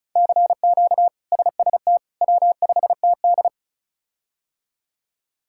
Morse code, heard as a keyed tone, about 36 wpm, 700 Hz. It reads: CQ SST W5TD
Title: K1USN SST QSO Practice - 35wpm